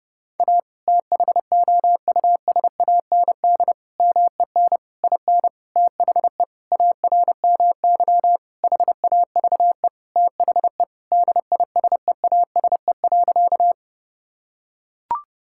Morse code, heard as a keyed tone, about 32 wpm, 700 Hz. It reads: A THOUSAND MEN IN THE ARMY HAVE THE DISEASE.
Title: Sentences from Top 1000 Words - Review Part 6 - 30wpm